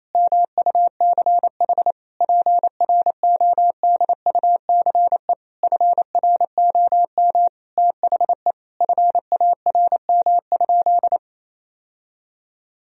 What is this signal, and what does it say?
Morse code, heard as a keyed tone, about 28 wpm, 700 Hz. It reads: MUCH PRODUCE FROM THE FARM?
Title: Sentences from Top 1000 Words - Review Part 3 - 28wpm